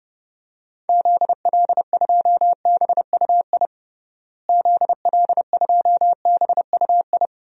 Morse code, heard as a keyed tone, about 30 wpm, 700 Hz. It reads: ZL2BUI ZL2BUI
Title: DXCC-100 Call Signs - 6 Characters - Repeated 2x - 30wpm